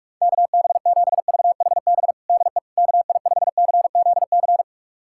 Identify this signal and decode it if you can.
Morse code, heard as a keyed tone, about 47 wpm, 690 Hz. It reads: KB7VHB DE KI5CZC
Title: USA Callsign DE Callsign - 6 Characters - 45wpm